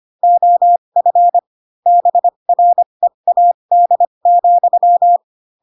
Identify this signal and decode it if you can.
Morse code, heard as a keyed tone, about 25 wpm, spaced as for 22 wpm, 700 Hz. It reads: OF BREAD,